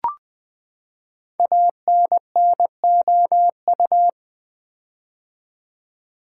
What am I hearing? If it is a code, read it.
Morse code, heard as a keyed tone, about 20 wpm, 700 Hz. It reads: ANNOU
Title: Build A Word - 5 Letters At A Time - 12-Letter Words - 20wpm